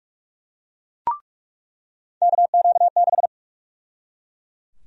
Morse code, heard as a keyed tone, about 43 wpm, 700 Hz. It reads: KQ6
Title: USA Call Sign Prefixes - 45wpm